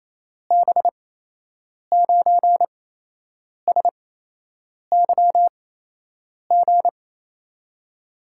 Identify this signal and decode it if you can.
Morse code, heard as a keyed tone, about 28 wpm, spaced as for 8 wpm, 700 Hz. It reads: B9SYG
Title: USA Call Signs - 6 Characters - 8x Character Spacing - 28wpm